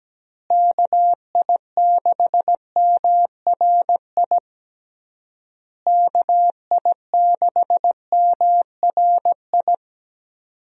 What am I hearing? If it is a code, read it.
Morse code, heard as a keyed tone, about 17 wpm, 700 Hz. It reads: KI6MRI KI6MRI
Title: USA Call Signs - 6 Characters - Repeated 3x - 17wpm